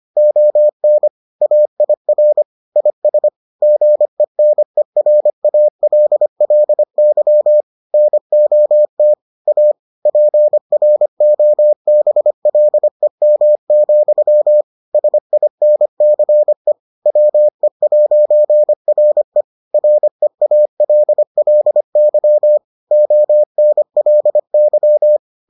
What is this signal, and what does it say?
Morse code, heard as a keyed tone, about 25 wpm, 600 Hz. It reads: ON AIR IS GENERALLY NOT A PROBLEM, SINCE WE'RE REALLY ONLY